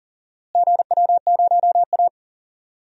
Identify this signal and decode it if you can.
Morse code, heard as a keyed tone, about 41 wpm, 700 Hz. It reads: GW0A
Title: DXCC-100 Call Signs - 4 Characters - Repeated 2x - 40wpm